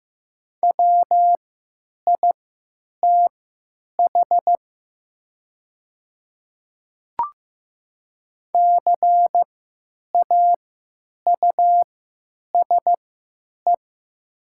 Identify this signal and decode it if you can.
Morse code, heard as a keyed tone, about 15 wpm, spaced as for 9 wpm, 700 Hz. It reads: WITH CAUSE
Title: 3x Character Spacing - Top 200 Words - 15wpm